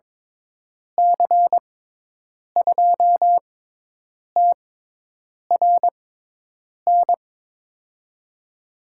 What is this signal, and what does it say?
Morse code, heard as a keyed tone, about 21 wpm, spaced as for 8 wpm, 700 Hz. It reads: C2TRN
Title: USA Call Signs - 6x Character Spacing - 22wpm